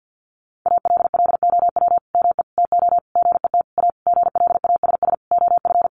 Morse code, heard as a keyed tone, about 50 wpm, 700 Hz. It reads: ALLOW ME TO GET A GLASS OF
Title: Sentences from Top 1300 Words 50wpm